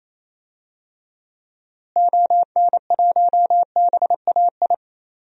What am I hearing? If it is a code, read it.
Morse code, heard as a keyed tone, about 28 wpm, 700 Hz. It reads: ON1BAI
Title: DXCC-100 Call Signs - 6 Characters - Repeated 2x - 28wpm